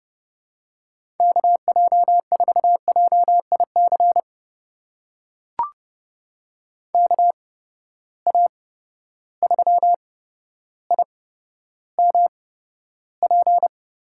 Morse code, heard as a keyed tone, about 30 wpm, spaced as for 13 wpm, 700 Hz. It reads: KJ4JIC KA3IMP
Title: USA Call Signs - 6 Characters - 8x Character Spacing - 30wpm